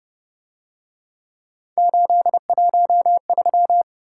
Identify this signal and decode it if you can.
Morse code, heard as a keyed tone, about 28 wpm, 700 Hz. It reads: 813